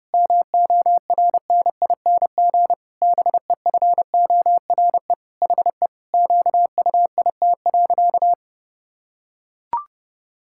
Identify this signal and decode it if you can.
Morse code, heard as a keyed tone, about 30 wpm, 700 Hz. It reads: MORNING BEFORE HE QUIT.
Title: Pangram Challenge v2 - 30wpm